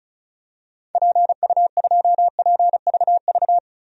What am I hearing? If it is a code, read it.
Morse code, heard as a keyed tone, about 34 wpm, 700 Hz. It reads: PU2PVV
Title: DXCC-100 Call Signs - 6 Characters - 35wpm